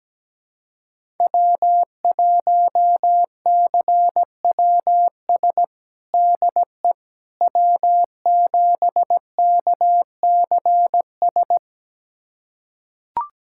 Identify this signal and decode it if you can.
Morse code, heard as a keyed tone, about 17 wpm, 700 Hz. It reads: W1CWS DE W7KCS
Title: USA Callsign DE Callsign - 5 Characters - 17wpm